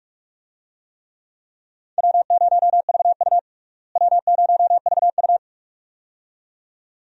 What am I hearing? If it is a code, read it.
Morse code, heard as a keyed tone, about 46 wpm, 700 Hz. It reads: W0VU W0VU
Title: USA Call Signs - 4 Characters - Repeated 2x - 45wpm